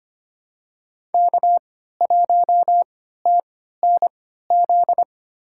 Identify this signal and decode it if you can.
Morse code, heard as a keyed tone, about 25 wpm, spaced as for 14 wpm, 700 Hz. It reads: K1TNZ
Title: USA Call Signs - 5 Characters - 3x Character Spacing - 25wpm